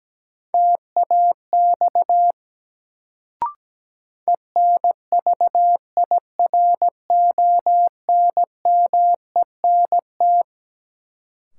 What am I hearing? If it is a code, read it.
Morse code, heard as a keyed tone, about 17 wpm, 700 Hz. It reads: TAX ENVIRONMENT
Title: Sentences from Top 1000 Words - Review Part 5 - New Word Drill - 17wpm